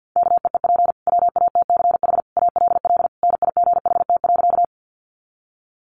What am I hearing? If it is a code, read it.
Morse code, heard as a keyed tone, about 50 wpm, 700 Hz. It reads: KEEP WATCH ALL NIGHT.
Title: Sentences from Top 300 Words 50wpm